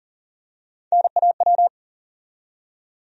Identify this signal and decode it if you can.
Morse code, heard as a keyed tone, about 39 wpm, 700 Hz. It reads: NAW
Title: USA Call Sign Postfixes - 3 Letters 40wpm